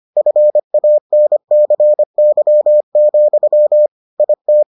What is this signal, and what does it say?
Morse code, heard as a keyed tone, about 25 wpm, 600 Hz. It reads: FANCY, IT